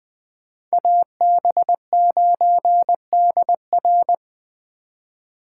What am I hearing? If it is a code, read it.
Morse code, heard as a keyed tone, about 24 wpm, 700 Hz. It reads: AB9DR
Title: Speed Racing - USA 2x2 Call Signs 20wpm